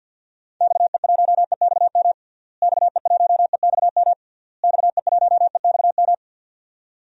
Morse code, heard as a keyed tone, about 51 wpm, 700 Hz. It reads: XE1EXM XE1EXM XE1EXM
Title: DXCC-100 Call Signs - 6 Characters - Repeated 3x - 50wpm